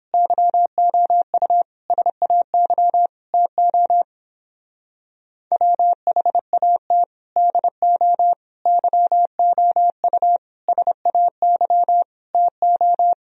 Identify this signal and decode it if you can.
Morse code, heard as a keyed tone, about 29 wpm, 700 Hz. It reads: YOU SAY TO WHAT DO YOU SAY TO